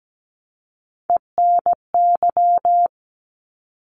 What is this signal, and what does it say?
Morse code, heard as a keyed tone, about 17 wpm, 700 Hz. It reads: ENY
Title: ARRL Sections 17wpm